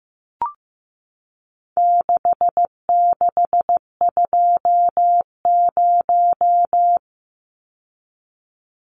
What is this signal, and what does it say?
Morse code, heard as a keyed tone, about 15 wpm, 700 Hz. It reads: 6620
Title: Numbers - 4-Digit 15wpm